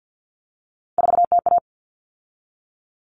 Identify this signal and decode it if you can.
Morse code, heard as a keyed tone, about 50 wpm, 700 Hz. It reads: 4TA
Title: All Cut Numbers - 3-Digit 50wpm